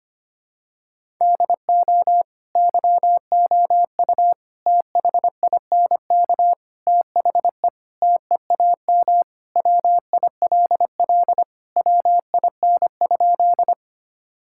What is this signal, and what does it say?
Morse code, heard as a keyed tone, about 25 wpm, 700 Hz. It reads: DO YOU THINK THE TEAM WILL WIN?
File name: practice / 25wpm / Sentences with 4-Word Phrases - 25wpm